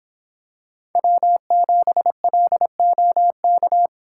Morse code, heard as a keyed tone, about 23 wpm, 700 Hz. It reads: W7LOK